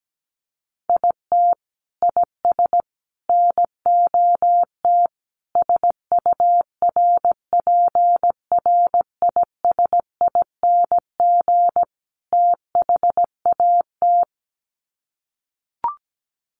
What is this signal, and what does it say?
Morse code, heard as a keyed tone, about 17 wpm, 700 Hz. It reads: IT IS NOT SURPRISING THAT
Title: Sets of 5 Words 17wpm